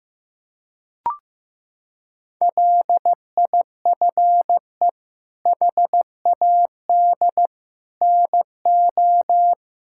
Morse code, heard as a keyed tone, about 15 wpm, 700 Hz. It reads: LIFE HAD NO
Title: Sentences from Top 1000 Words - Review Part 6 - 15wpm